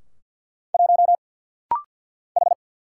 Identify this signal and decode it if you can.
Morse code, heard as a keyed tone, about 50 wpm, 690 Hz. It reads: 1 H
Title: Single Letter-Number Alt - Rapid-Fire 50wpm